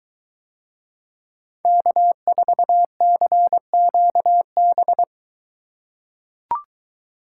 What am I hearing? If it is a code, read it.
Morse code, heard as a keyed tone, about 23 wpm, 700 Hz. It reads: K4CQB